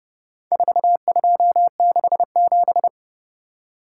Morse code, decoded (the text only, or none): 4267